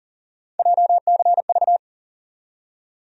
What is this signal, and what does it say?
Morse code, heard as a keyed tone, about 40 wpm, 700 Hz. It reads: JCV
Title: USA Call Sign Postfixes - 3 Letters 40wpm